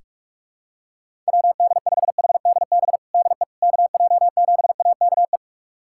Morse code, heard as a keyed tone, about 42 wpm, 700 Hz. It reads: WD5HDB DE KJ7AKE
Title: USA Callsign DE Callsign - 6 Characters - 45wpm